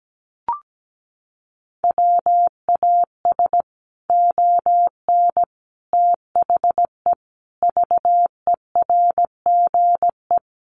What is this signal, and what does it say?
Morse code, heard as a keyed tone, about 17 wpm, 700 Hz. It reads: WAS ON THE VERGE
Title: Sets of 5 Words Encore 17wpm